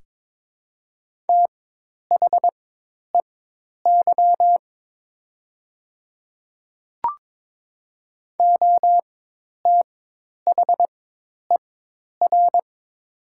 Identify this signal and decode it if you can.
Morse code, heard as a keyed tone, about 22 wpm, spaced as for 10 wpm, 700 Hz. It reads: THEY OTHER
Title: 4x Character Spacing - Top 300 Words - 22wpm